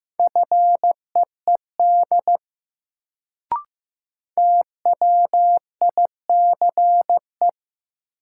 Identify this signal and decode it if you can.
Morse code, heard as a keyed tone, about 15 wpm, 700 Hz. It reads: FEED TWICE
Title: Sentences from Top 1600 Words - New Word Drill - 15wpm